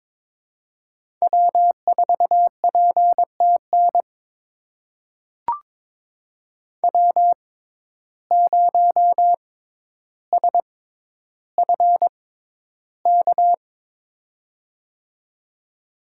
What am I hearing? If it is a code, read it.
Morse code, heard as a keyed tone, about 21 wpm, spaced as for 11 wpm, 700 Hz. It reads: W4PTN W0SFK
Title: USA Call Signs - 5 Characters - 6x Character Spacing - 22wpm